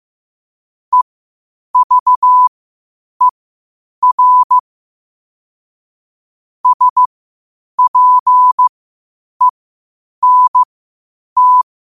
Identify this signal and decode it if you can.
Morse code, heard as a keyed tone, about 15 wpm, spaced as for 8 wpm, 1000 Hz. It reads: EVER SPENT
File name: qotdfiles / 8wpm